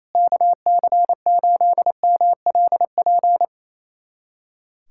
Morse code, heard as a keyed tone, about 28 wpm, 700 Hz. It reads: KC8MLP
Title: USA Callsign DE Callsign 28wpm